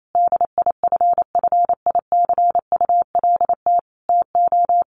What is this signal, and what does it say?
Morse code, heard as a keyed tone, about 28 wpm, 700 Hz. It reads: DIFFICULT TO
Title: Sentences from Top 1500 Words 28wpm